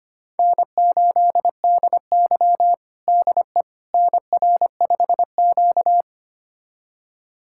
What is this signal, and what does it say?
Morse code, heard as a keyed tone, about 25 wpm, 700 Hz. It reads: N8DY DE NR5Q
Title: USA Callsign DE Callsign - 4 Characters - 25wpm